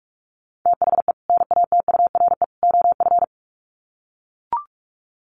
Morse code, heard as a keyed tone, about 44 wpm, 700 Hz. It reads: THE NATURE OF
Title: Sets of 3 Words 45wpm